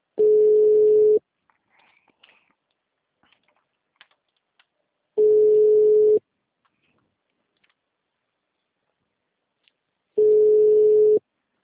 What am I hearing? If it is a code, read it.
Ringback tone (Europe, 425 Hz).
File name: out- -2002- - - .8475